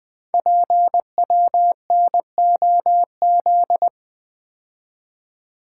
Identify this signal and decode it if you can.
Morse code, heard as a keyed tone, about 20 wpm, 700 Hz. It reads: PWNOZ
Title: Code Groups - 5 Letters - 20wpm